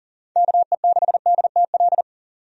Morse code, heard as a keyed tone, about 40 wpm, 700 Hz. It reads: KE6DTL
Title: USA Call Signs - 6 Characters - 40wpm